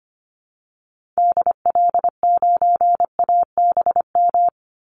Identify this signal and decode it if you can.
Morse code, heard as a keyed tone, about 25 wpm, 700 Hz. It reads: DL9ABM
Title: Call Signs - 25wpm